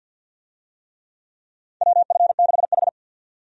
Morse code, heard as a keyed tone, about 51 wpm, 700 Hz. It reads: WF6H
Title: USA Call Signs - 4 Characters - 50wpm